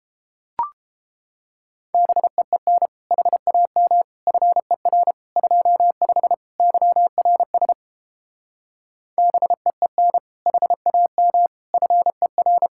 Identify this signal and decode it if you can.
Morse code, heard as a keyed tone, about 30 wpm, 700 Hz. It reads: BEEN HAM FER 25 YRS BEEN HAM FER